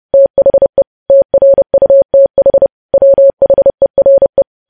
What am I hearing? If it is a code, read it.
Morse code, heard as a keyed tone, about 30 wpm, 550 Hz. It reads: THE TRUTH WHERE
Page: news headlines in morse code at 30 wpm